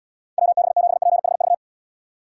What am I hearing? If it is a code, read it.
Morse code, heard as a keyed tone, about 74 wpm, 700 Hz. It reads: WB7PHV